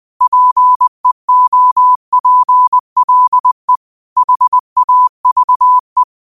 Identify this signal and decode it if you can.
Morse code, heard as a keyed tone, about 20 wpm, 1000 Hz. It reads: PEOPLE HAVE
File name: qotdfiles / 20wpm